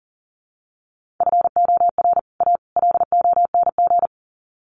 Morse code, heard as a keyed tone, about 40 wpm, 700 Hz. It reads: FOR A LONG
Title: Sets of 3 Words 40wpm